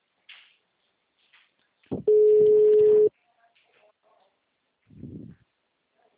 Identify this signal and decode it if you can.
Ringback tone (Europe, 425 Hz).